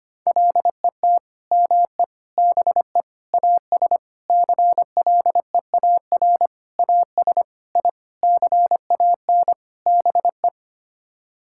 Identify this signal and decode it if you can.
Morse code, heard as a keyed tone, about 25 wpm, 700 Hz. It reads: LET ME BE AS CLEAR AS I CAN BE
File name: Sets of 9 Words - 25wpm